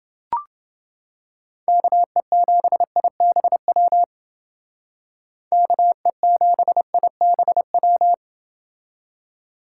Morse code, heard as a keyed tone, about 29 wpm, 700 Hz. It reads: KE7IBW KE7IBW